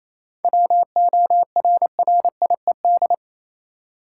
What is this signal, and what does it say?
Morse code, heard as a keyed tone, about 27 wpm, 700 Hz. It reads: WORRIED
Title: Seven Letter Words - 28wpm